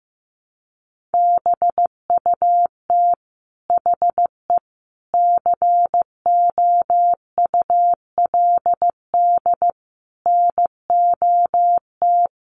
Morse code, heard as a keyed tone, about 15 wpm, 700 Hz. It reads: BUT HE COULD NOT
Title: Sentences from Top 600 Words 15wpm